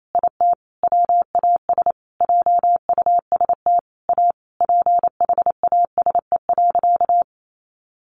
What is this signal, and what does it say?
Morse code, heard as a keyed tone, about 28 wpm, 700 Hz. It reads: IT WAS JUST A PHASE.
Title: Sentences from Top 1200 Words 28wpm